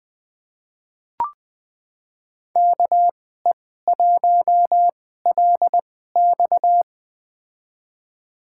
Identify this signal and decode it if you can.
Morse code, heard as a keyed tone, about 20 wpm, spaced as for 14 wpm, 700 Hz. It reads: KE1LX
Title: USA Call Signs - 5 Characters - 2x Character Spacing - 20wpm